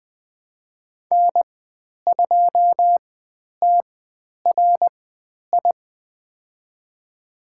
Morse code, heard as a keyed tone, about 20 wpm, spaced as for 10 wpm, 700 Hz. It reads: N2TRI
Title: 095 Lesson 16g - Letter c - Call Signs 20wpm